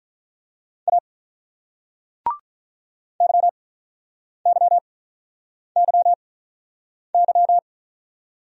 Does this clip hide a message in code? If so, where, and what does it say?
Yes: Morse code from 0.9 to 7.6 s, heard as a keyed tone, about 47 wpm, spaced as for 9 wpm, 700 Hz. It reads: A YYYY